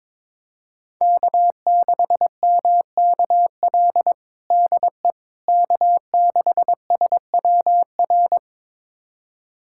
Morse code, heard as a keyed tone, about 22 wpm, 700 Hz. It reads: K6MKL DE K6SWR
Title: USA Callsign DE Callsign - 5 Characters - 22wpm